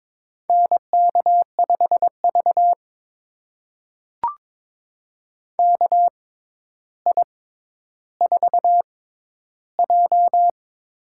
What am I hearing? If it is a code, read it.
Morse code, heard as a keyed tone, about 22 wpm, spaced as for 11 wpm, 700 Hz. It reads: NK5V KI4J
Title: USA Call Signs - 4 Characters - 6x Character Spacing - 22wpm